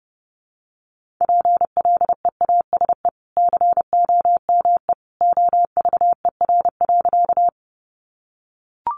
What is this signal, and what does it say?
Morse code, heard as a keyed tone, about 30 wpm, 700 Hz. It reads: PLEASE COME OVER.
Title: Sentences from Top 1500 Words 30wpm